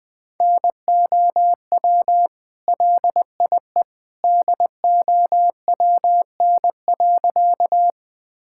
Morse code, heard as a keyed tone, about 19 wpm, 700 Hz. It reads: NOW LIE DOWN.
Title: Sentences from Top 1600 Words 20wpm